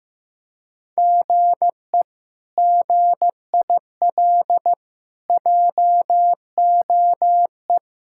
Morse code, heard as a keyed tone, about 15 wpm, 700 Hz. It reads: GE GIL JOE
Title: K1USN SST QSO Practice - 15wpm